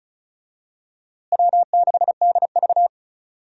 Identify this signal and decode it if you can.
Morse code, heard as a keyed tone, about 36 wpm, 700 Hz. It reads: W6DV
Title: USA Call Signs - 4 Characters - Repeated 2x - 35wpm